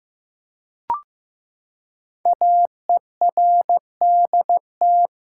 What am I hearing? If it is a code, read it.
Morse code, heard as a keyed tone, about 15 wpm, 700 Hz. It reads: AERDT
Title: Code Groups - 5 Letters - 15wpm